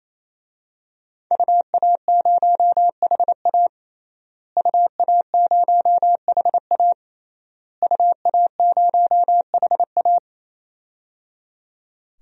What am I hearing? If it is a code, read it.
Morse code, heard as a keyed tone, about 28 wpm, 700 Hz. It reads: UA0HA UA0HA UA0HA
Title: DXCC-100 Call Signs - 5 Characters - Repeated 3x - 28wpm